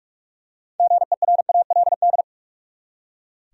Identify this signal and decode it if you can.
Morse code, heard as a keyed tone, about 46 wpm, 700 Hz. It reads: GERALD
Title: USA Common Names - 45wpm